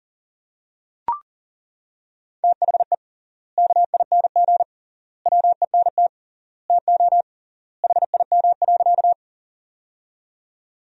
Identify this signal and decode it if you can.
Morse code, heard as a keyed tone, about 40 wpm, 700 Hz. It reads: THE KING WENT TO HIM.
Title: 3x Word-Spacing - Sentences from Top 400 Words - 40wpm